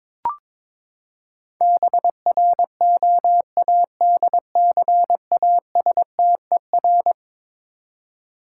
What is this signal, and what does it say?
Morse code, heard as a keyed tone, about 22 wpm, 700 Hz. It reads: BROADCASTER
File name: practice / 22wpm / Eleven Letter Words - 22wpm